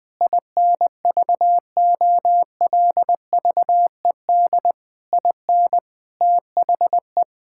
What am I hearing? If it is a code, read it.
Morse code, heard as a keyed tone, about 21 wpm, 700 Hz. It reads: INVOLVED IN THE
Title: Sentences from Top 1000 Words - Review Part 5 - 20wpm